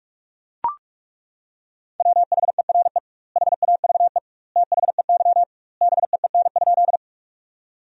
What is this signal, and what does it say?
Morse code, heard as a keyed tone, about 41 wpm, 700 Hz. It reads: WHERE HAVE THEY BEEN?
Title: 2x Word-Spacing - Sentences from Top 200 Words - 45wpm